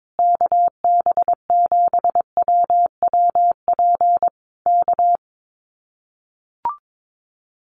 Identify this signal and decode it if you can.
Morse code, heard as a keyed tone, about 22 wpm, 700 Hz. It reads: KB7WWP K
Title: Common QSO Phrases Encore 22wpm